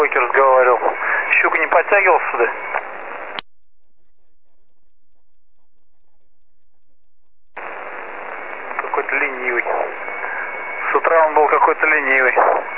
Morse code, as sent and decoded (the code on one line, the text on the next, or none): none